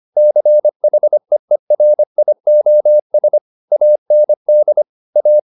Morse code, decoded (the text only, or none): CHEERIOS AND A